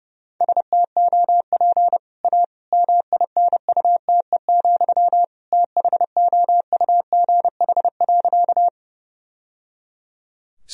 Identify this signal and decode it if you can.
Morse code, heard as a keyed tone, about 32 wpm, 700 Hz. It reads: STOP A MINUTE, THOUGH.
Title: Sentences from Top 1000 Words - Review Part 2 - 30wpm